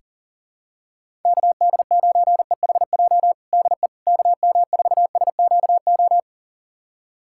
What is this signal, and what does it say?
Morse code, heard as a keyed tone, about 37 wpm, 700 Hz. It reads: KD9EHJ DE KM4SQO
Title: USA Callsign DE Callsign 40wpm